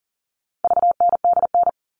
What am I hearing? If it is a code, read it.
Morse code, heard as a keyed tone, about 40 wpm, 700 Hz. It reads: VNDN
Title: All Cut Numbers - 4-Digit 40wpm